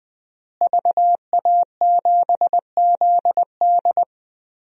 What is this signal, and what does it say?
Morse code, heard as a keyed tone, about 20 wpm, 700 Hz. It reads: VA7ZD
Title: DXCC-100 Call Signs - 5 Characters - Repeated 3x - 20wpm